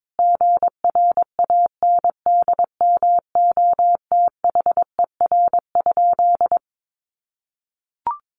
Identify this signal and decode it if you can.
Morse code, heard as a keyed tone, about 22 wpm, 700 Hz. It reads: GRANDMOTHER?
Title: Sentences from Top 1200 Words 22wpm